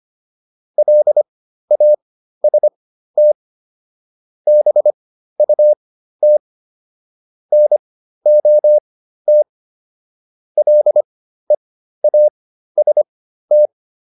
Morse code, heard as a keyed tone, about 25 wpm, spaced as for 13 wpm, 600 Hz. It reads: LAST BUT NOT LEAST
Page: intermediate-practice-files